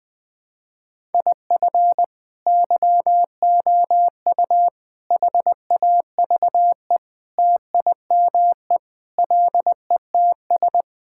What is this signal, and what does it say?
Morse code, heard as a keyed tone, about 21 wpm, 700 Hz. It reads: IF YOU HAVE TIME LETS
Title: Sentences with 3-Word Phrases - 20wpm